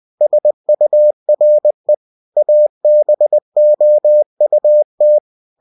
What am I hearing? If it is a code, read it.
Morse code, heard as a keyed tone, about 20 wpm, 600 Hz. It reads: SURE ABOUT
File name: cw-books / winnie-the-pooh / winnie-20wpm-page-0065